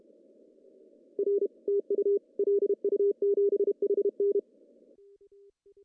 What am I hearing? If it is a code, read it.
Morse code, heard as a keyed tone, about 30 wpm, 410 Hz. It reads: R TU LU7HN
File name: CW / cqww-cw-k5tr